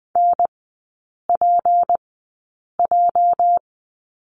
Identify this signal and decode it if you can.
Morse code, heard as a keyed tone, about 20 wpm, spaced as for 8 wpm, 700 Hz. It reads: NPJ
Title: 205 Lesson 33e - Letter j - Three Characters 20wpm